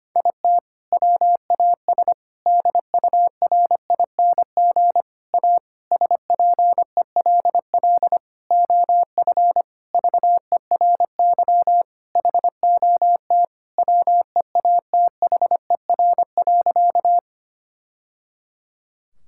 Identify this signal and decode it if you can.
Morse code, heard as a keyed tone, about 25 wpm, 700 Hz. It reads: IT WAS DURING A SPELL OF VERY HOT WEATHER.
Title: Sentences from Top 1000 Words - Review Part 5 - 25wpm